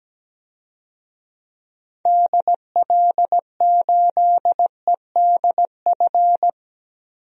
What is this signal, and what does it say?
Morse code, heard as a keyed tone, about 17 wpm, 700 Hz. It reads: DL8EDF
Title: DXCC-100 Call Signs - 6 Characters - Repeated 3x - 17wpm